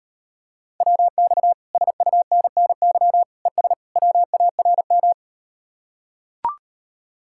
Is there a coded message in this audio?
Yes: Morse code from 0.8 to 5.1 s, heard as a keyed tone, about 38 wpm, 700 Hz. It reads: WX SUNNY ES WARM